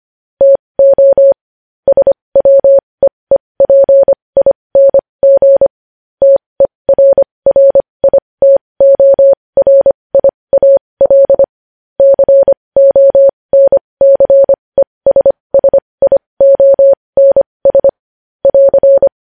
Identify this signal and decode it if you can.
Morse code, heard as a keyed tone, about 25 wpm, spaced as for 20 wpm, 550 Hz. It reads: TO SWEEPING TERRITORIAL CONCESSIONS <AR>
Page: news headlines in morse code at 20 wpm